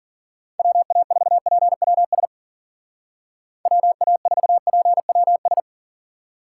Morse code, heard as a keyed tone, about 50 wpm, 700 Hz. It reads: WA4PWS WA4PWS